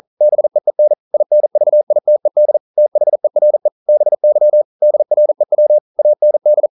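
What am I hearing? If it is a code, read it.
Morse code, heard as a keyed tone, about 40 wpm, 600 Hz. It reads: BEEN INVITED THERE BY DREW AND
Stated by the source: Morse code practice at 41 wpm, from CW med Gnister.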